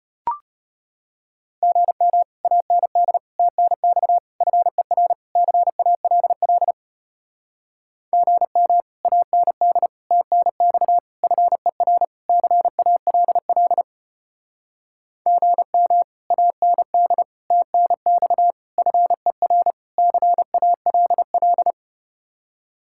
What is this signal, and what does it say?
Morse code, heard as a keyed tone, about 33 wpm, 700 Hz. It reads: GM AND TNX FER CALL GM AND TNX FER CALL GM AND TNX FER CALL